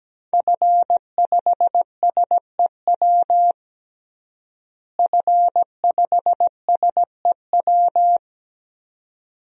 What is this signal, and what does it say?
Morse code, heard as a keyed tone, about 17 wpm, 700 Hz. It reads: F5SEW F5SEW
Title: DXCC-100 Call Signs - 5 Characters - Repeated 2x - 17wpm